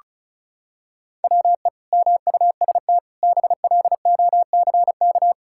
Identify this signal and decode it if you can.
Morse code, heard as a keyed tone, about 34 wpm, 700 Hz. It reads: WE MUST BLOCK
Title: Sentences from Top 1600 Words 35wpm